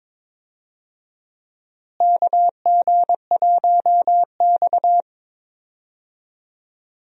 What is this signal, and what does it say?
Morse code, heard as a keyed tone, about 22 wpm, 700 Hz. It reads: KG1X
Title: USA Call Signs - 4 Characters - Repeated 2x - 22wpm